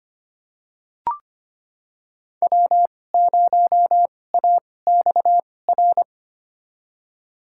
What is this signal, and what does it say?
Morse code, heard as a keyed tone, about 25 wpm, spaced as for 18 wpm, 700 Hz. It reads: W0AXR
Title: USA Call Signs - 5 Characters - 2x Character Spacing - 25wpm